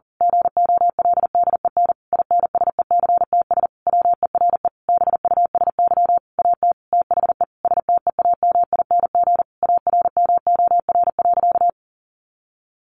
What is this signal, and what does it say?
Morse code, heard as a keyed tone, about 40 wpm, 700 Hz. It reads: GOLDEN INSECTS WERE BUSY AT THE STEAMING ARMOR.
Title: A Sound Of Thunder 40wpm